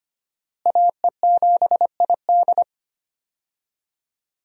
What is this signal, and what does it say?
Morse code, heard as a keyed tone, about 25 wpm, 700 Hz. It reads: AE7ID
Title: USA Call Signs - 5 Characters - Repeated 3x - 25wpm